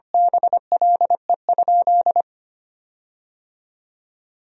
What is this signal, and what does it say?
Morse code, heard as a keyed tone, about 25 wpm, 700 Hz. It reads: BLE?
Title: 4x Word-Spacing - Sentences from Top 500 Words - 25wpm